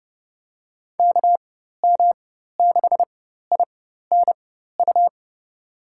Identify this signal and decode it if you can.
Morse code, heard as a keyed tone, about 32 wpm, spaced as for 14 wpm, 700 Hz. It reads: KM6INU
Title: USA Call Signs - 6 Characters - 4x Character Spacing - 30wpm